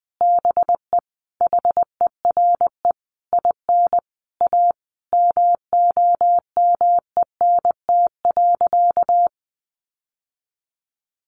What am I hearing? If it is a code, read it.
Morse code, heard as a keyed tone, about 19 wpm, 700 Hz. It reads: BE HERE IN A MOMENT.
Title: Sentences from Top 1000 Words 20wpm